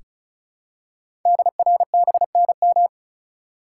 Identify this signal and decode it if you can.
Morse code, heard as a keyed tone, about 36 wpm, 700 Hz. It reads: DRBNM
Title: Code Groups - 5 Letters - 35wpm